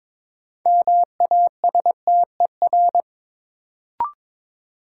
Morse code, heard as a keyed tone, about 22 wpm, 700 Hz. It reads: MASTER
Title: Sentences from Top 1300 Words - New Word Drill - 22wpm